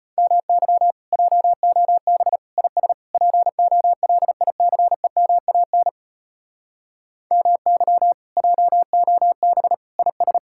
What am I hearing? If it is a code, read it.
Morse code, heard as a keyed tone, about 33 wpm, 700 Hz. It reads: MY JOB IS POLICEMAN MY JOB IS